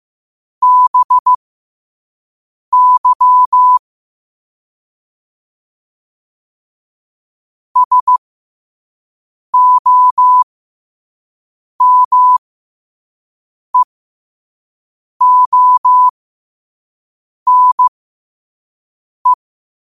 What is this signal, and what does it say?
Morse code, heard as a keyed tone, about 15 wpm, spaced as for 5 wpm, 1000 Hz. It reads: BY SOMEONE
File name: qotdfiles / 5wpm